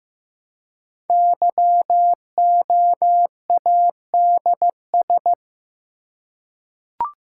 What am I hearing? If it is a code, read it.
Morse code, heard as a keyed tone, about 15 wpm, 700 Hz. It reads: YOADS
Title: Code Groups - 5 Letter:Numbers - 15wpm